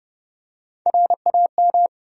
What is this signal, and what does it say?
Morse code, heard as a keyed tone, about 30 wpm, 700 Hz. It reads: RAM
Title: USA Call Sign Postfixes - 3 Letters 30wpm